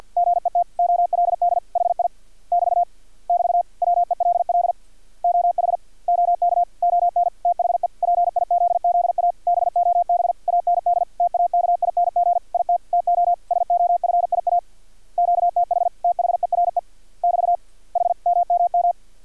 Morse code, heard as a keyed tone, about 50 wpm, 700 Hz. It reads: GET OLD HI <BT> <BT> WELL OH OK ON THE PIZZA BOB AND TAKING IT TO SOFIA QTH THERE <BT> HMMM
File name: archived / CRSnet